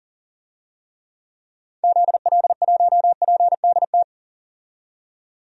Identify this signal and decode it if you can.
Morse code, heard as a keyed tone, about 40 wpm, 700 Hz. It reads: ZL1PDT
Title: DXCC-100 Call Signs - 6 Characters - 40wpm